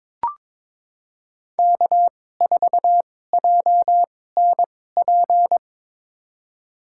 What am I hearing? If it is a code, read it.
Morse code, heard as a keyed tone, about 22 wpm, spaced as for 16 wpm, 700 Hz. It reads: K4JNP
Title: USA Call Signs - 5 Characters - 2x Character Spacing - 22wpm